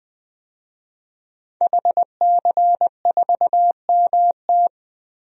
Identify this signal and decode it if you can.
Morse code, heard as a keyed tone, about 20 wpm, 700 Hz. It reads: HC4MT
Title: DXCC-100 Call Signs - 5 Characters - 20wpm